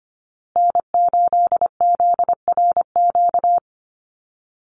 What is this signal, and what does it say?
Morse code, heard as a keyed tone, about 25 wpm, 700 Hz. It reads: N8ZRQ
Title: USA Call Signs Repeated 3x 25wpm